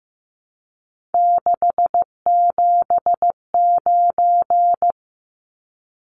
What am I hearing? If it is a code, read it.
Morse code, heard as a keyed tone, about 15 wpm, 700 Hz. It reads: 679
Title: Numbers - 3-Digit 15wpm